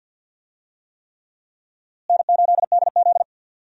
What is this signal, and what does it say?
Morse code, heard as a keyed tone, about 52 wpm, 700 Hz. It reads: N8DZ
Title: USA Call Signs - 4 Characters - Repeated 3x - 50wpm